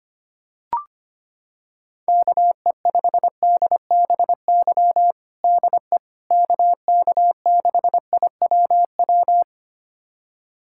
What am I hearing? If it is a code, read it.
Morse code, heard as a keyed tone, about 25 wpm, 700 Hz. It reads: KE5DBY DE KK6IWW
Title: USA Callsign DE Callsign - 6 Characters - 25wpm